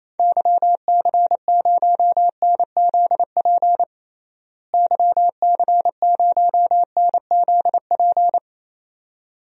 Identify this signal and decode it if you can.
Morse code, heard as a keyed tone, about 28 wpm, 700 Hz. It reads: YC0NZP YC0NZP
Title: DXCC-100 Call Signs - 6 Characters - Repeated 2x - 28wpm